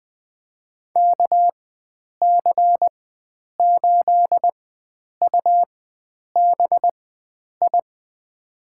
Morse code, heard as a keyed tone, about 20 wpm, spaced as for 9 wpm, 700 Hz. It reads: KC8UBI
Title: USA Call Signs - 6 Characters - 4x Character Spacing - 20wpm